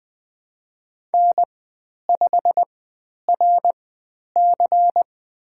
Morse code, heard as a keyed tone, about 20 wpm, spaced as for 10 wpm, 700 Hz. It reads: N5RC
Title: 102 Lesson 17g - Letter u - Call Signs 20wpm